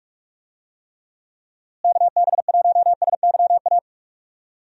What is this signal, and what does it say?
Morse code, heard as a keyed tone, about 44 wpm, 700 Hz. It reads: KB1SYA
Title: USA Call Signs - 6 Characters - Repeated 2x - 45wpm